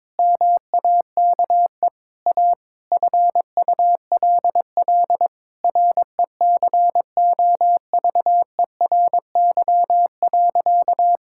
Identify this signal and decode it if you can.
Morse code, heard as a keyed tone, about 22 wpm, 700 Hz. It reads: MAKE A FULL RECOVERY.
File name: Sentences with 7-Word Phrases - 22wpm